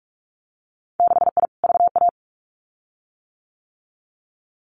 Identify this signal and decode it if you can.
Morse code, heard as a keyed tone, about 44 wpm, 700 Hz. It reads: 6I VA
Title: Winter Field Day Exchanges Encore 45wpm